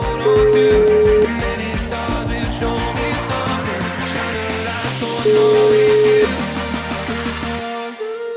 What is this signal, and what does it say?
Ringback tone (Europe, 425 Hz), with noise.